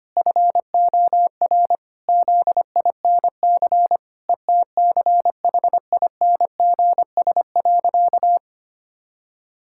Morse code, heard as a keyed tone, about 25 wpm, 700 Hz. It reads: FOR ZINC ETCHINGS.
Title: Pangram Challenge v2 - 25wpm